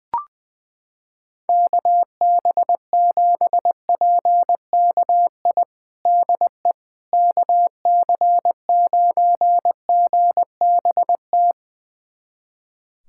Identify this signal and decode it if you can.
Morse code, heard as a keyed tone, about 20 wpm, 700 Hz. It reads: KB7PKI DE KC9GBT
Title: USA Callsign DE Callsign - 6 Characters - 20wpm